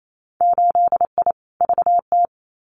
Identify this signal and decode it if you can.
Morse code, heard as a keyed tone, about 28 wpm, 700 Hz. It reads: 8I VT
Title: Winter Field Day Exchanges Encore 28wpm